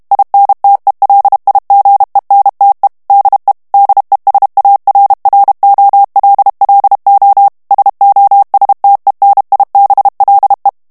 Morse code, heard as a keyed tone, about 32 wpm, 800 Hz. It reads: INTELIGENTE DE DESARROLLO SOSTENIBLE
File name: morse mp3 / ESPAGNOL TEXTE EN CLAIR 1 VITESSE 30 MOTS MINUTES